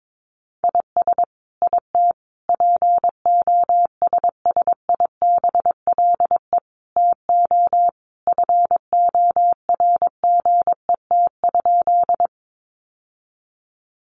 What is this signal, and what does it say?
Morse code, heard as a keyed tone, about 22 wpm, 700 Hz. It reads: IS IT POSSIBLE TO FORGET?
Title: Sentences from Top 1200 Words 22wpm